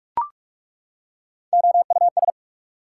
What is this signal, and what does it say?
Morse code, heard as a keyed tone, about 48 wpm, 690 Hz. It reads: OUS
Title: Prefixes and Suffixes 45wpm